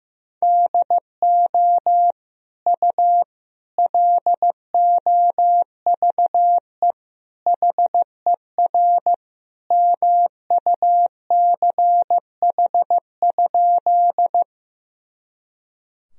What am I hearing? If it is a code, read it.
Morse code, heard as a keyed tone, about 16 wpm, 700 Hz. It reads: DO U LOVE HER MUCH?
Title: Sentences from Top 400 Words and Abbreviations 15wpm